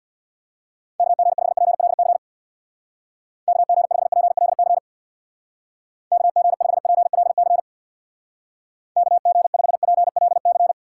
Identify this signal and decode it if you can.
Morse code, heard as a keyed tone, about 65 wpm, 700 Hz. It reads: KG5PLC KG5PLC KG5PLC KG5PLC